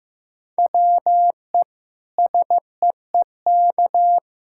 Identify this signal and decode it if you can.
Morse code, heard as a keyed tone, about 15 wpm, 700 Hz. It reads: WE SEEK
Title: Sentences from Top 1600 Words 15wpm